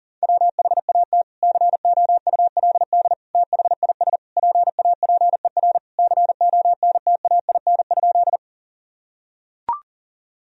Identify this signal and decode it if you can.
Morse code, heard as a keyed tone, about 40 wpm, 700 Hz. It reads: WHAT COULD THIS PAPER CONTAIN?
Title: Sentences from Top 1000 Words - Review Part 2 - 40wpm